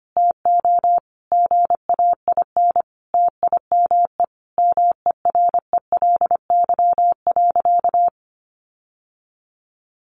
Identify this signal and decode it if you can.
Morse code, heard as a keyed tone, about 25 wpm, 700 Hz. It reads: TO GAIN TIME MERELY.
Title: Sentences from Top 1400 Words 25wpm